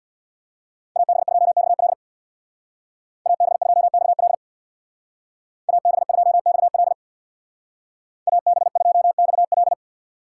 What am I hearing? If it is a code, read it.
Morse code, heard as a keyed tone, about 64 wpm, 700 Hz. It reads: AB2XL AB2XL AB2XL AB2XL